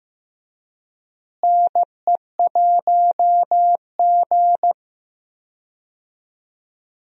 Morse code, heard as a keyed tone, about 15 wpm, 700 Hz. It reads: NE1G
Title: USA Call Signs - 4 Characters - Repeated 2x - 15wpm